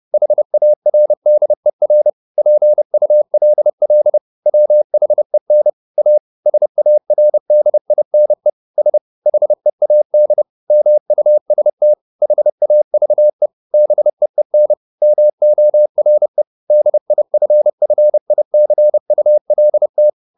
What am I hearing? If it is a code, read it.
Morse code, heard as a keyed tone, about 30 wpm, 600 Hz. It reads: HARDER PULL WHEN A SARDINE S HEAD MUST HAVE BEEN MORE DIFFICULT
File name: the old man and the sea / the old man and the sea 1 30wpm